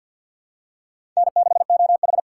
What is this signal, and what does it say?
Morse code, heard as a keyed tone, about 52 wpm, 700 Hz. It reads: N6OH
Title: USA Call Signs - 4 Characters - 2x Character Spacing - 50wpm